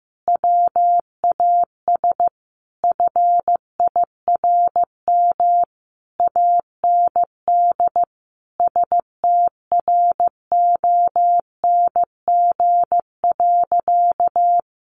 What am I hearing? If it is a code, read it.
Morse code, heard as a keyed tone, about 15 wpm, 700 Hz. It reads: WAS FIRM AND STRONG.
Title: Sentences from Top 1400 Words 15wpm